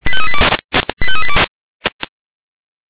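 Five-tone selective call (ZVEI), heard as quick runs of five tones, tones of about 70 ms. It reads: 85382
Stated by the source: ZVEI five-tone call 85380 received by SDR.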